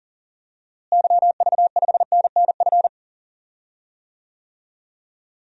Morse code, heard as a keyed tone, about 40 wpm, 700 Hz. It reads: YV5NNF
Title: DXCC-100 Call Signs - 6 Characters - Repeated 3x - 40wpm